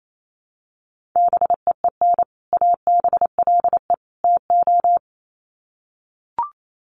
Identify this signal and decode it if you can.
Morse code, heard as a keyed tone, about 28 wpm, 700 Hz. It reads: BEEN ABLE TO
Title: Sets of 3 Words 28wpm